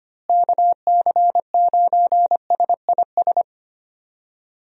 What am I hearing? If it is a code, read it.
Morse code, heard as a keyed tone, about 25 wpm, 700 Hz. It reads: KC9SIS
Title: USA Call Signs - 6 Characters - Repeated 2x - 25wpm